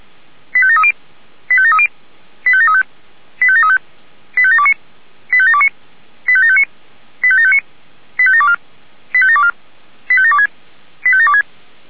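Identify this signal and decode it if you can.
Five-tone selective call (ZVEI), heard as quick runs of five tones, tones of about 70 ms. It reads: 85730 85730 85735 85735 85729 85729 85759 85759 85724 85724 85726 85726
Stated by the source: ZVEI five-tone selective call received on an SDR.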